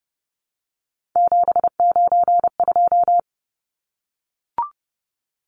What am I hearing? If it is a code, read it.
Morse code, heard as a keyed tone, about 29 wpm, 700 Hz. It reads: 792
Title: Numbers - 3-Digit 30wpm